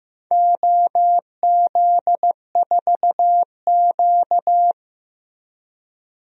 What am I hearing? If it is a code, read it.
Morse code, heard as a keyed tone, about 15 wpm, 700 Hz. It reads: OZ4Q
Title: DXCC-100 Call Signs - 4 Characters - Repeated 3x - 15wpm